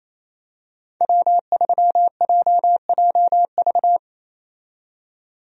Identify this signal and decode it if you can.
Morse code, heard as a keyed tone, about 28 wpm, 700 Hz. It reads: W3JJV
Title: USA Call Signs - 5 Characters - 28wpm